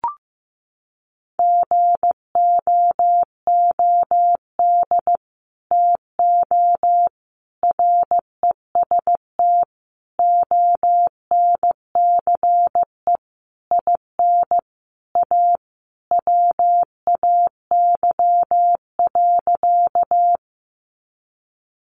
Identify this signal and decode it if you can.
Morse code, heard as a keyed tone, about 15 wpm, 700 Hz. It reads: GOOD TO REST ONCE IN A WAY.
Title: Sentences from Top 500 Words 15wpm